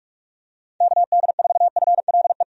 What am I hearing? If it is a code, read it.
Morse code, heard as a keyed tone, about 43 wpm, 700 Hz. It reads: KD4FLE
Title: USA Call Signs - 6 Characters - 45wpm